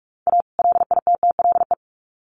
Morse code, heard as a keyed tone, about 44 wpm, 700 Hz. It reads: A LITTLE
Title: Sets of 2 Words 45wpm